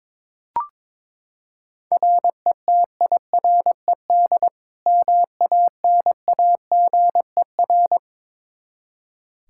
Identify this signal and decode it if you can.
Morse code, heard as a keyed tone, about 23 wpm, 700 Hz. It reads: RETIRED MANAGER
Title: Practice QSO - Exchange 3 - More Info - 22wpm